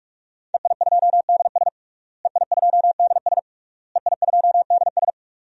Morse code, heard as a keyed tone, about 41 wpm, 700 Hz. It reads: EI2DS EI2DS EI2DS
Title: DXCC-100 Call Signs - 5 Characters - Repeated 3x - 45wpm